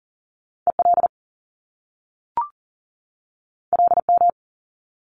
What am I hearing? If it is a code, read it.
Morse code, heard as a keyed tone, about 39 wpm, 700 Hz. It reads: EL LM
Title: Top Two Letter Combinations in Words 40wpm